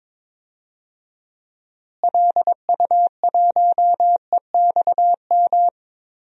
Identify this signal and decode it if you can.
Morse code, heard as a keyed tone, about 22 wpm, 700 Hz. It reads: LU1EXM
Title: DXCC-100 Call Signs - 6 Characters - Repeated 3x - 22wpm